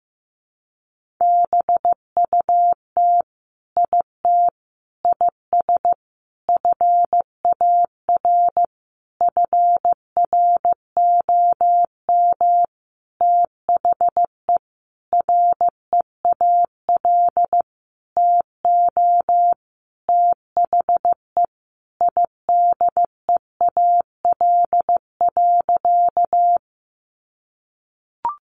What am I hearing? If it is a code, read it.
Morse code, heard as a keyed tone, about 15 wpm, 700 Hz. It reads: BUT IT IS FAR FROM THE REAL TO THE IDEAL.
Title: Sentences from Top 1400 Words 15wpm